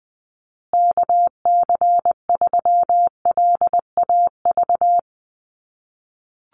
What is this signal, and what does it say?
Morse code, heard as a keyed tone, about 20 wpm, 700 Hz. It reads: KC3LAV
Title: USA Call Signs Repeated 3x 20wpm